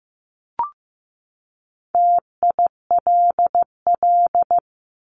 Morse code, heard as a keyed tone, about 15 wpm, 700 Hz. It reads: TILL
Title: Top 1751-2000 Words 15wpm